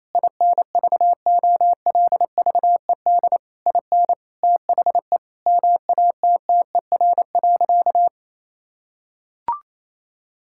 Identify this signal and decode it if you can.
Morse code, heard as a keyed tone, about 28 wpm, 700 Hz. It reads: INVOLVED IN THE MATTER.
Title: Sentences from Top 1000 Words - Review Part 5 - 28wpm